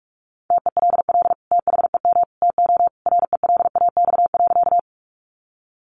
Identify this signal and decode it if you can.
Morse code, heard as a keyed tone, about 44 wpm, 700 Hz. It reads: TELL THEM TO RELAX.
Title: Sentences from Top 1400 Words 45wpm